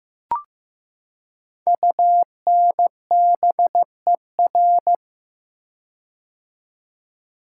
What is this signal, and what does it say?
Morse code, heard as a keyed tone, about 15 wpm, 700 Hz. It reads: UNBER
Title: Code Groups - 5 Letters - 15wpm